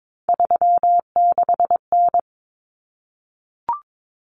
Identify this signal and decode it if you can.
Morse code, heard as a keyed tone, about 22 wpm, 700 Hz. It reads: 36N
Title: Common Cut Numbers - 3-Digit 22wpm